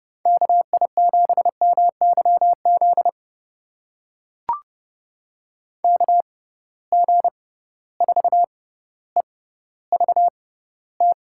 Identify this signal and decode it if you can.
Morse code, heard as a keyed tone, about 30 wpm, spaced as for 15 wpm, 700 Hz. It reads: KI7MYZ KG4EVT
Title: USA Call Signs - 6 Characters - 6x Character Spacing - 30wpm